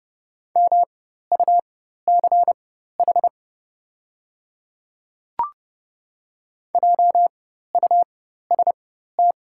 Morse code, heard as a keyed tone, about 30 wpm, spaced as for 14 wpm, 700 Hz. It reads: MUCH JUST
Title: 4x Character Spacing - Top 200 Words - 30wpm